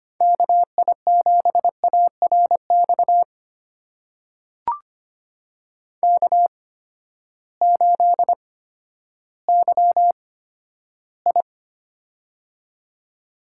Morse code, heard as a keyed tone, about 25 wpm, spaced as for 12 wpm, 700 Hz. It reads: KI7ARX K8YI
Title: USA Call Signs - 8x Character Spacing - 25wpm